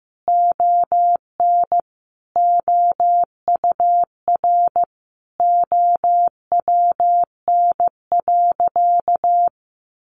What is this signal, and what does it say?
Morse code, heard as a keyed tone, about 15 wpm, 700 Hz. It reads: ON OUR OWN.
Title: Sentences from Top 1500 Words 15wpm